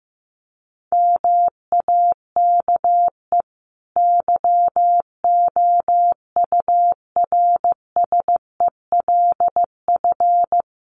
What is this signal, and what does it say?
Morse code, heard as a keyed tone, about 15 wpm, 700 Hz. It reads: MAKE YOURSELF
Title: Sentences from Top 600 Words 15wpm